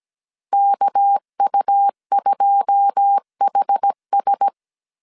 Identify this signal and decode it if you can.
Morse code, heard as a keyed tone, about 17 wpm, 800 Hz. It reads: KU2HS
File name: CW / DET / 1 9 1500C S A 16wpm 2828